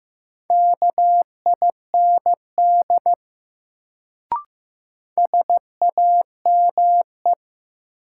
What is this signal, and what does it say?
Morse code, heard as a keyed tone, about 15 wpm, 700 Hz. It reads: KIND SAME
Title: Sentences from Top 200 Words - New Word Drill - 15wpm